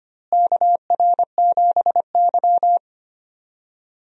Morse code, heard as a keyed tone, about 25 wpm, 700 Hz. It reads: KR7Y
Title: USA Call Signs - 4 Characters - Repeated 3x - 25wpm